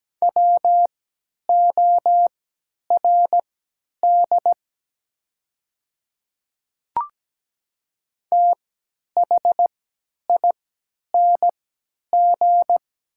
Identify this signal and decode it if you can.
Morse code, heard as a keyed tone, about 17 wpm, spaced as for 10 wpm, 700 Hz. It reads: WORD THING
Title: 3x Character Spacing - Top 100 Words - 17wpm